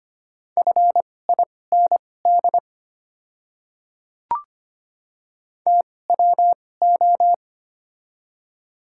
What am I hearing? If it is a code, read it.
Morse code, heard as a keyed tone, about 25 wpm, spaced as for 18 wpm, 690 Hz. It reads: FIND TWO
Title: 2x Character Spacing - Top 100 Words - 25wpm